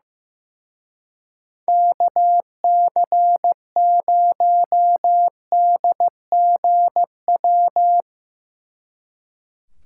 Morse code, heard as a keyed tone, about 15 wpm, 700 Hz. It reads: KC0DGW
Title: Speed Racing - USA 2x3 Call Signs 15wpm